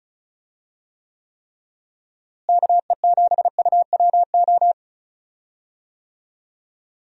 Morse code, heard as a keyed tone, about 36 wpm, 700 Hz. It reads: KE7UWO
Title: USA Call Signs - 6 Characters - Repeated 3x - 35wpm